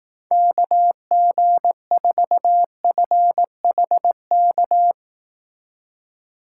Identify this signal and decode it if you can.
Morse code, heard as a keyed tone, about 18 wpm, 700 Hz. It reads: KG4FHK
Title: Speed Racing - USA 2x3 Call Signs 15wpm